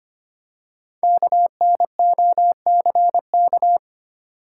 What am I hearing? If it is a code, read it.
Morse code, heard as a keyed tone, about 25 wpm, 700 Hz. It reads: KNOCK
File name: 5 Letter Words - First 3 Letters Given - 25wpm